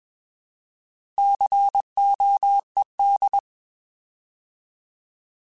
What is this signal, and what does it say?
Morse code, heard as a keyed tone, about 21 wpm, 800 Hz. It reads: COED
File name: Alphabetical / Alphabetical mp3s / Cs @20